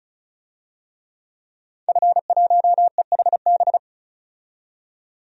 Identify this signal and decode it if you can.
Morse code, heard as a keyed tone, about 36 wpm, 700 Hz. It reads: F1EHB
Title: DXCC-100 Call Signs - 5 Characters - 35wpm